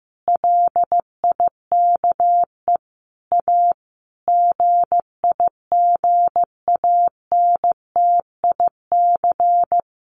Morse code, heard as a keyed tone, about 15 wpm, 700 Hz. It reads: LIKE A GIGANTIC
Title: A Sound Of Thunder 15wpm